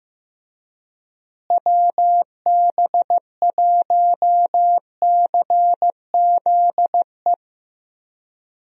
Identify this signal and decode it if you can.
Morse code, heard as a keyed tone, about 15 wpm, 700 Hz. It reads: WB1CZE
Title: USA Call Signs - 6 Characters - Repeated 3x - 15wpm